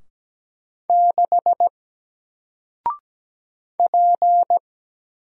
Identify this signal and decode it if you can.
Morse code, heard as a keyed tone, about 17 wpm, 700 Hz. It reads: 6 P
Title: Single Letter-Number 17wpm